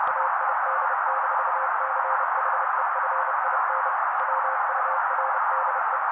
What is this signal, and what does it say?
Morse code, heard as a keyed tone, about 30 wpm, 530 Hz. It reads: AID-CHEFIN WURDE
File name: daily / 29wpm noise